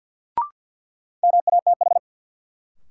Morse code, heard as a keyed tone, about 53 wpm, 700 Hz. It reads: MATH
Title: Sentences from Top 1000 Words - Review Part 6 - New Word Drill - 50wpm